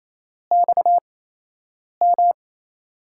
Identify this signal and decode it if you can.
Morse code, heard as a keyed tone, about 28 wpm, 700 Hz. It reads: X M
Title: USA Call Signs - 4 Characters - 8x Character Spacing - 28wpm